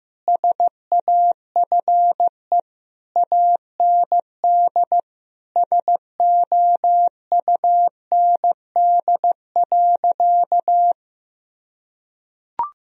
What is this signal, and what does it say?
Morse code, heard as a keyed tone, about 15 wpm, 700 Hz. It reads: SAFE AND SOUND.
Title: Sentences from Top 1000 Words - Review Part 1 - 15wpm